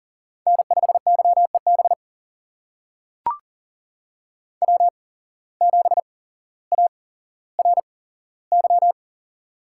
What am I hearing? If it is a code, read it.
Morse code, heard as a keyed tone, about 38 wpm, spaced as for 17 wpm, 700 Hz. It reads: N5YEB W7ARY
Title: USA Call Signs - 5 Characters - 8x Character Spacing - 40wpm